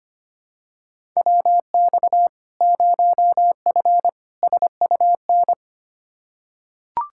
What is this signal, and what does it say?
Morse code, heard as a keyed tone, about 25 wpm, 700 Hz. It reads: WX 0F SUN
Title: Practice QSO - Exchange 2 - Info - 25wpm